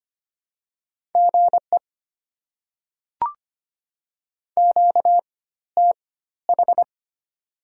Morse code, heard as a keyed tone, about 25 wpm, spaced as for 14 wpm, 700 Hz. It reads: GE QTH
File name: Top 20 QSO Elements - 4x Character Spacing - 25wpm